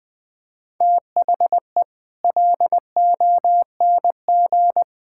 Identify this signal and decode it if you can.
Morse code, heard as a keyed tone, about 21 wpm, 700 Hz. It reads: THE LONG
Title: Sets of 9 Words - 20wpm